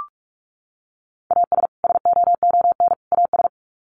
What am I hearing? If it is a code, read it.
Morse code, heard as a keyed tone, about 44 wpm, 700 Hz. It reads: AS SOON AS
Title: Sets of 3 Words 45wpm